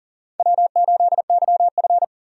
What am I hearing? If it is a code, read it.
Morse code, heard as a keyed tone, about 40 wpm, 700 Hz. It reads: W8YF
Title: USA Call Signs - 4 Characters - 40wpm